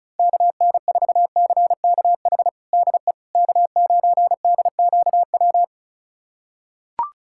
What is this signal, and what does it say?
Morse code, heard as a keyed tone, about 36 wpm, 700 Hz. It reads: KN4CKH DE K9DQW
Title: USA Callsign DE Callsign 35wpm